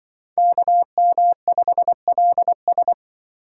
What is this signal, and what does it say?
Morse code, heard as a keyed tone, about 24 wpm, 700 Hz. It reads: KM5LS
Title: Speed Racing - USA 2x2 Call Signs 20wpm